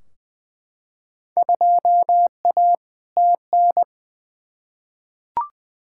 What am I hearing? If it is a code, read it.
Morse code, heard as a keyed tone, about 20 wpm, 700 Hz. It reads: 2A TN
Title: Field Day Exchanges - 20wpm